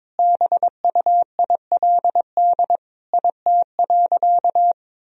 Morse code, heard as a keyed tone, about 22 wpm, 700 Hz. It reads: BUILD IT.
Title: Sentences from Top 1000 Words - Review Part 1 - 22wpm